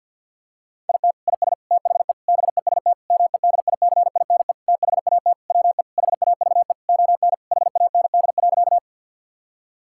Morse code, heard as a keyed tone, about 51 wpm, 700 Hz. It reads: IT IS THE BEST MEDICINE THAT WE HAVE ON HAND.
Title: Sentences from Top 1000 Words - Review Part 6 - 50wpm